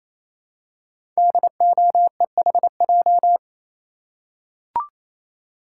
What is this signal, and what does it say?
Morse code, heard as a keyed tone, about 28 wpm, 700 Hz. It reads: DOEHJ
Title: Code Groups - 5 Letters - 28wpm